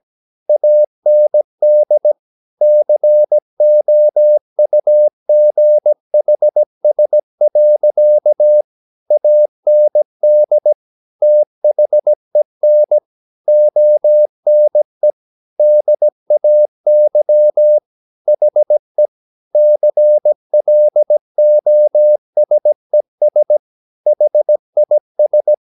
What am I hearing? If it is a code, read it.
Morse code, heard as a keyed tone, about 17 wpm, 600 Hz. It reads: AND COUGHS. AND THEN ONE DAY HE CLOSES HIS
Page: Romanen Never af Ken Follet på engelsk.